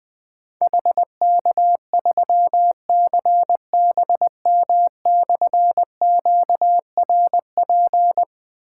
Morse code, heard as a keyed tone, about 20 wpm, 700 Hz. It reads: HK3CBM/QRP
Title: Difficult Call Signs 20wpm